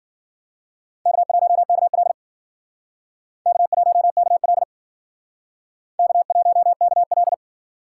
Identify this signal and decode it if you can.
Morse code, heard as a keyed tone, about 55 wpm, 700 Hz. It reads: K1KL K1KL K1KL